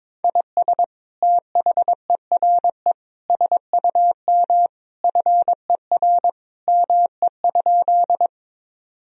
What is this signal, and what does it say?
Morse code, heard as a keyed tone, about 21 wpm, 700 Hz. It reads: IS THERE SUM FER ME?
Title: Sentences from Top 200 Words and Abbreviations 22wpm